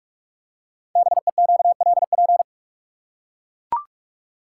Morse code, heard as a keyed tone, about 44 wpm, 690 Hz. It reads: BEQLP
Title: Code Groups - 5 Letters - 45wpm